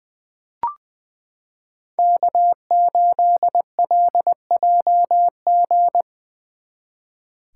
Morse code, heard as a keyed tone, about 20 wpm, 700 Hz. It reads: K8LJG
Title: 2023 SOTA Chaser Call Signs - 20wpm